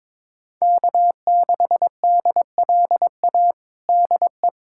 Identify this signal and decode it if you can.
Morse code, heard as a keyed tone, about 21 wpm, 700 Hz. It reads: K6DLA DE
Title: USA Callsign DE Callsign - 5 Characters - 22wpm